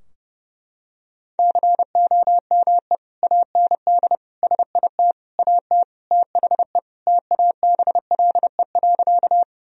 Morse code, heard as a keyed tone, about 30 wpm, 700 Hz. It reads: COME AND SIT AT THE TABLE.
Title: Sentences from Top 1000 Words - Review Part 2 - 30wpm